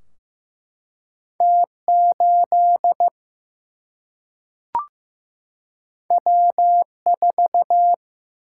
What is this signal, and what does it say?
Morse code, heard as a keyed tone, about 15 wpm, 700 Hz. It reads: T 8 W 4
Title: Single-Letter Single-Number 15wpm